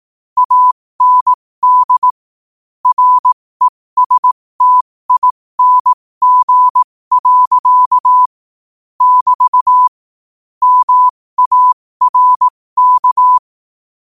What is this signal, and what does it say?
Morse code, heard as a keyed tone, about 18 wpm, spaced as for 15 wpm, 1000 Hz. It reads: AND RESTING. <BT> MARK
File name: qotdfiles / 15wpm